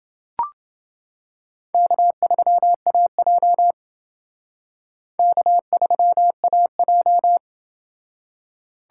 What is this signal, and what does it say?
Morse code, heard as a keyed tone, about 28 wpm, 700 Hz. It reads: K3AJ K3AJ